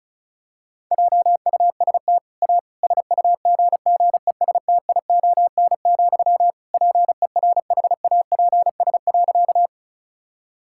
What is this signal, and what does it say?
Morse code, heard as a keyed tone, about 36 wpm, 700 Hz. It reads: JUST A SUGGESTION, PERHAPS.
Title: Sentences from Top 1600 Words 35wpm